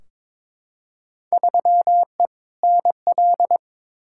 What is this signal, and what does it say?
Morse code, heard as a keyed tone, about 21 wpm, 700 Hz. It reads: 3E NL
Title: Field Day Exchanges Encore - 22wpm